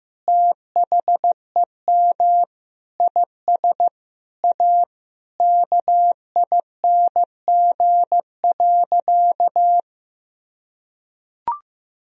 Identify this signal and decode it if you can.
Morse code, heard as a keyed tone, about 16 wpm, 700 Hz. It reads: THEM IS A KING.
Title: Sentences from Top 1000 Words - Review Part 3 - 15wpm